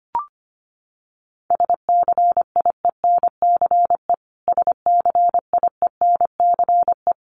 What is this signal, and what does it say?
Morse code, heard as a keyed tone, about 24 wpm, 700 Hz. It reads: SCIENCE SCIENCE
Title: Top 500 Words Repeated 25wpm